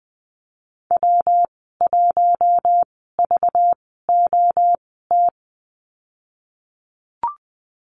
Morse code, heard as a keyed tone, about 19 wpm, spaced as for 14 wpm, 700 Hz. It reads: W1VOT
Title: 200 Lesson 32g - Letter k - Call Signs 20wpm